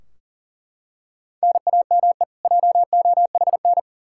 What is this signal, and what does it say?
Morse code, heard as a keyed tone, about 40 wpm, 700 Hz. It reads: NAME JOHN
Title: Practice QSO - Exchange 1 - Beginning - 40wpm